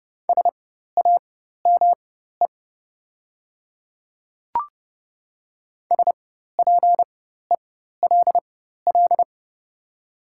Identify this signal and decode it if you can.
Morse code, heard as a keyed tone, about 30 wpm, spaced as for 14 wpm, 700 Hz. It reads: SAME SPELL
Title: 4x Character Spacing - Top 300 Words - 30wpm